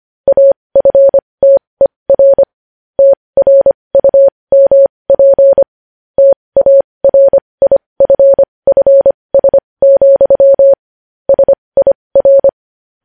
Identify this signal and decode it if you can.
Morse code, heard as a keyed tone, about 25 wpm, spaced as for 20 wpm, 550 Hz. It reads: AFTER TRUMP TARIFFS, SIR
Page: News Headlines in Morse Code at 20 WPM